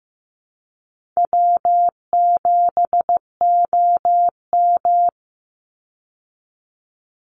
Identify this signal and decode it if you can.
Morse code, heard as a keyed tone, about 15 wpm, 700 Hz. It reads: W7OM
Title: Easy Call Signs Repeated 2x - 15wpm